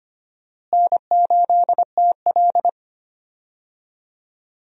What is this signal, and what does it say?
Morse code, heard as a keyed tone, about 25 wpm, 700 Hz. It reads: N8TL
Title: USA Call Signs - 4 Characters - Repeated 3x - 25wpm